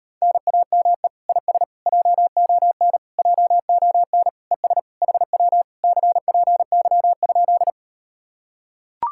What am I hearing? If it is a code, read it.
Morse code, heard as a keyed tone, about 39 wpm, 700 Hz. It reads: NAME IS JON JON ES HW CPY?